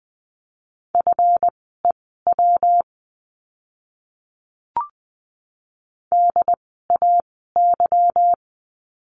Morse code, heard as a keyed tone, about 20 wpm, spaced as for 14 wpm, 700 Hz. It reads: FEW DAY
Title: 206 Lesson 33f - Letter j - Words 20wpm